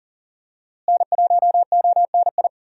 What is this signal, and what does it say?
Morse code, heard as a keyed tone, about 40 wpm, 700 Hz. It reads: N1ONI
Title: USA Call Signs - 5 Characters - Repeated 3x - 40wpm